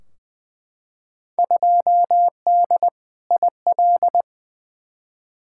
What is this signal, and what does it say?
Morse code, heard as a keyed tone, about 20 wpm, 700 Hz. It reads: 2D IL
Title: Field Day Exchanges - 20wpm